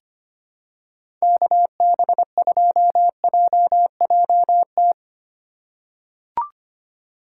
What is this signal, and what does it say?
Morse code, heard as a keyed tone, about 25 wpm, 700 Hz. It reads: KB2JJT
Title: USA Call Signs - 6 Characters - 2x Character Spacing - 25wpm